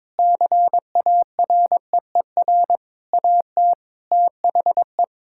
Morse code, heard as a keyed tone, about 22 wpm, 700 Hz. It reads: CAREER AT THE
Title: Sentences from Top 1000 Words - Review Part 6 - 22wpm